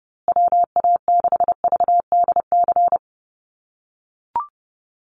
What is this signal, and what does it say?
Morse code, heard as a keyed tone, about 30 wpm, 700 Hz. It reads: WA6VDC
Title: USA Call Signs Repeated 2x 30wpm